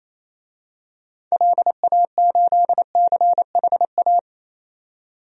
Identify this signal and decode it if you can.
Morse code, heard as a keyed tone, about 28 wpm, 700 Hz. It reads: LA8CHA
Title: DXCC-100 Call Signs - 6 Characters - 28wpm